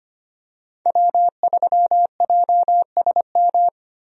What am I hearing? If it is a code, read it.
Morse code, heard as a keyed tone, about 25 wpm, 700 Hz. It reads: W3JSM
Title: USA Call Signs - 5 Characters - Repeated 3x - 25wpm